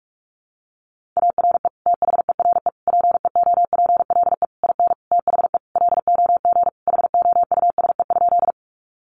Morse code, heard as a keyed tone, about 44 wpm, 700 Hz. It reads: ARE THERE PEOPLE IN THE LOG HOUSE?
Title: Sentences from Top 1400 Words 45wpm